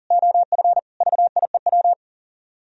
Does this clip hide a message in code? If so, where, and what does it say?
Yes: Morse code from 0.1 to 1.9 s, heard as a keyed tone, about 53 wpm, 700 Hz. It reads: OF VIEW